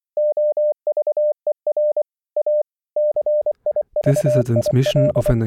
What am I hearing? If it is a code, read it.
Morse code, heard as a keyed tone, about 25 wpm, 600 Hz, with speech. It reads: OVER A CIVIL